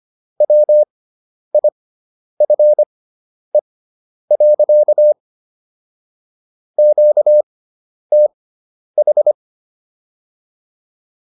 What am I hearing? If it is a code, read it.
Morse code, heard as a keyed tone, about 25 wpm, spaced as for 10 wpm, 600 Hz. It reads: WIFE. QTH